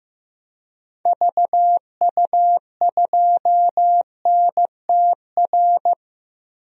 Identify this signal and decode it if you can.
Morse code, heard as a keyed tone, about 15 wpm, 700 Hz. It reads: VU2NTR
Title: DXCC-100 Call Signs - 6 Characters - 15wpm